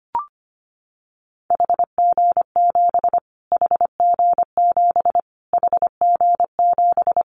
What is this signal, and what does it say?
Morse code, heard as a keyed tone, about 25 wpm, 700 Hz. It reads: HG7 HG7 HG7
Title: Callsign Prefixes for Top 25 DXCC - Repeated 3x - 25wpm